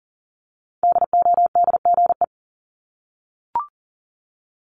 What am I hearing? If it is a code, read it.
Morse code, heard as a keyed tone, about 39 wpm, 700 Hz. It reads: DODGE
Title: Multi-letter Phonemes 40wpm